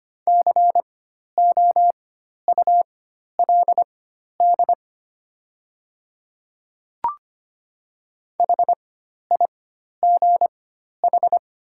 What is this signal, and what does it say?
Morse code, heard as a keyed tone, about 25 wpm, spaced as for 12 wpm, 700 Hz. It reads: COULD HIGH
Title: 4x Character Spacing - Top 300 Words - 25wpm